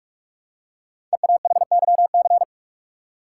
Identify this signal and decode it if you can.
Morse code, heard as a keyed tone, about 44 wpm, 700 Hz. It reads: EAHYC
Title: Code Groups - 5 Letters - 45wpm